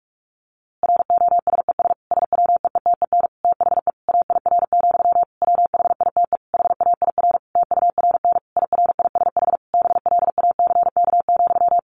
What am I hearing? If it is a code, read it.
Morse code, heard as a keyed tone, about 46 wpm, 700 Hz. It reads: ROSES SWEETEN THE AIR, WHITE HAIR TURN IRISH BLACK,
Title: A Sound Of Thunder 45wpm